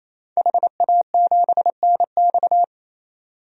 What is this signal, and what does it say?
Morse code, heard as a keyed tone, about 28 wpm, 700 Hz. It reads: HA7NX
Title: DXCC-100 Call Signs - 5 Characters - 28wpm